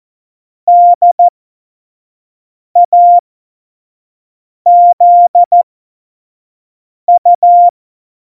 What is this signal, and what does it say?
Morse code, heard as a keyed tone, about 14 wpm, spaced as for 5 wpm, 700 Hz. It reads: DAZU
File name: Morsekurs / Witze / Witz 03. November 2020, 25 F70